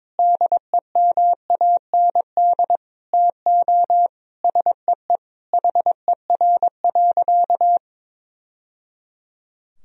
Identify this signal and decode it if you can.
Morse code, heard as a keyed tone, about 22 wpm, 700 Hz. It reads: DEMAND TO SEE HER.
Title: Sentences from Top 1000 Words - Review Part 4 - 22wpm